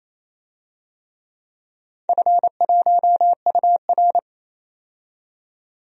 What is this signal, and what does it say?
Morse code, heard as a keyed tone, about 27 wpm, 700 Hz. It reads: F1UR
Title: DXCC-100 Call Signs - 4 Characters - 28wpm